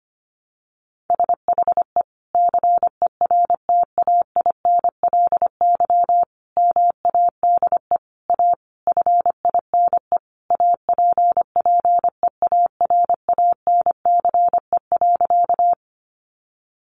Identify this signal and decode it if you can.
Morse code, heard as a keyed tone, about 25 wpm, 700 Hz. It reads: SHE CERTAINLY MADE A FINE APPEARANCE.
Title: Sentences from Top 1100 Words 25wpm